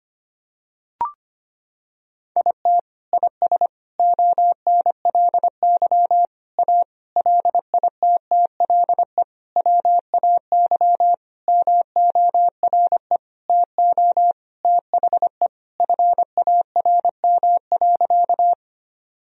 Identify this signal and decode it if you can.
Morse code, heard as a keyed tone, about 25 wpm, 700 Hz. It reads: IT IS ONLY A LITTLE WAY MORE TO THE FARM.
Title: Sentences from Top 1000 Words - Review Part 2 - 25wpm